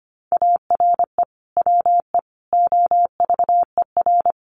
Morse code, heard as a keyed tone, about 25 wpm, 700 Hz. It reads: ARE WE OVER
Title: Sentences from Top 1200 Words 25wpm